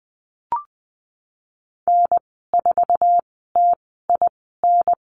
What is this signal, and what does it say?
Morse code, heard as a keyed tone, about 20 wpm, 700 Hz. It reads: N 4 T I N
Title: 055 Lesson 10g - Letter r - Call Signs 20wpm